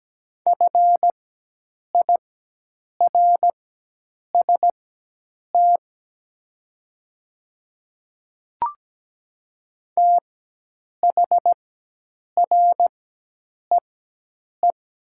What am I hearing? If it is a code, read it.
Morse code, heard as a keyed tone, about 17 wpm, spaced as for 8 wpm, 700 Hz. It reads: FIRST THREE
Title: 4x Character Spacing - Top 200 Words - 17wpm